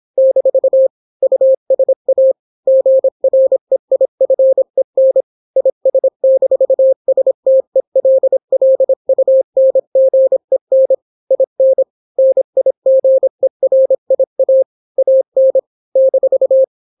Morse code, heard as a keyed tone, about 26 wpm, 530 Hz. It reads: - USA GREIFEN IS-STELLUNGEN IN NIGERIA AN -